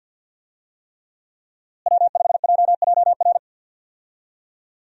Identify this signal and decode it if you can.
Morse code, heard as a keyed tone, about 52 wpm, 700 Hz. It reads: W5JJR
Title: USA Call Signs - 5 Characters - Repeated 3x - 50wpm